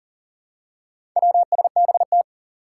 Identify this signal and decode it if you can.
Morse code, heard as a keyed tone, about 42 wpm, 700 Hz. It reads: WSBT
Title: Four Challenging Letter-Number Combinations 40wpm